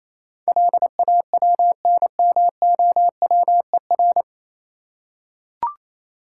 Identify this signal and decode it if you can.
Morse code, heard as a keyed tone, about 27 wpm, 690 Hz. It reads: LAWNMOWER
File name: Easy Words - Speed-Racing with Hints - 22wpm